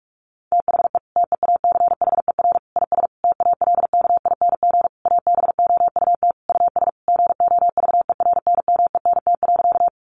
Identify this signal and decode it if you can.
Morse code, heard as a keyed tone, about 46 wpm, 700 Hz. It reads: THE TEACHER IS TALKING ABOUT US GOVERNMENT.
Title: Easy Phrases 45wpm